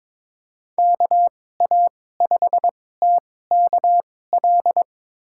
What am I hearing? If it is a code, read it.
Morse code, heard as a keyed tone, about 22 wpm, spaced as for 16 wpm, 700 Hz. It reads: KA5TKL
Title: USA Call Signs - 6 Characters - 2x Character Spacing - 22wpm